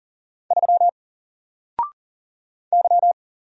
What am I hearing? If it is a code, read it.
Morse code, heard as a keyed tone, about 40 wpm, 700 Hz. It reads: 3 Y
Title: Single Letter-Number Alt 40wpm